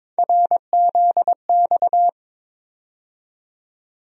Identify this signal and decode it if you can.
Morse code, heard as a keyed tone, about 22 wpm, 700 Hz. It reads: RZX
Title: USA Call Sign Postfixes - 3 Letters 22wpm